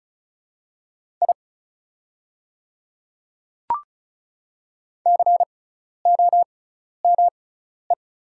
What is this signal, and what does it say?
Morse code, heard as a keyed tone, about 34 wpm, spaced as for 12 wpm, 700 Hz. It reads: I COME
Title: 6x Character Spacing - Top 200 Words - 35wpm